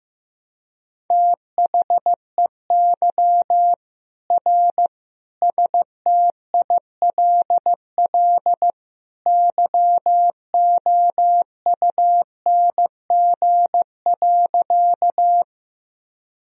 Morse code, heard as a keyed tone, about 15 wpm, 700 Hz. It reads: THEY R STILL YOUNG.
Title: Sentences from Top 400 Words and Abbreviations 15wpm